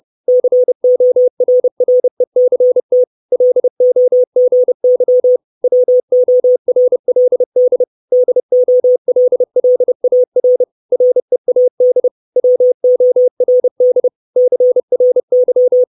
Morse code, heard as a keyed tone, about 30 wpm, 500 Hz. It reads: CORRECT LOGY WORLD DOLLAR READ WORD CRY
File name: 632 Words 30WPM